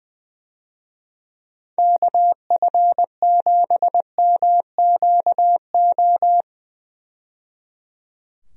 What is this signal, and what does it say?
Morse code, heard as a keyed tone, about 20 wpm, 700 Hz. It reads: KF7MQO
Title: USA Call Signs - 6 Characters - 20wpm